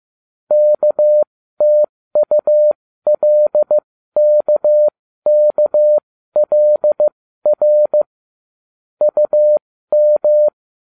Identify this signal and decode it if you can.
Morse code, heard as a keyed tone, about 15 wpm, spaced as for 13 wpm, 600 Hz. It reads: KTULKKLR UM